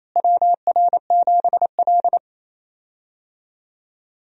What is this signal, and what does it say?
Morse code, heard as a keyed tone, about 28 wpm, 700 Hz. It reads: WR7L
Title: USA Call Signs - 4 Characters - Repeated 3x - 28wpm